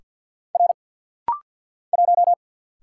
Morse code, heard as a keyed tone, about 50 wpm, 690 Hz. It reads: R 1
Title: Single Letter-Number Alt - Rapid-Fire 50wpm